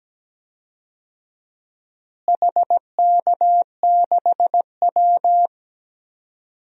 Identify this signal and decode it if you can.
Morse code, heard as a keyed tone, about 17 wpm, 700 Hz. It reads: HK6W
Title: DXCC-100 Call Signs - 4 Characters - Repeated 2x - 17wpm